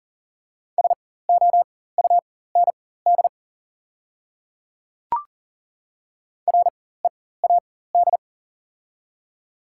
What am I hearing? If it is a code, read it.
Morse code, heard as a keyed tone, about 39 wpm, spaced as for 19 wpm, 690 Hz. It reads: SOUND READ
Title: 4x Character Spacing - Top 200 Words - 40wpm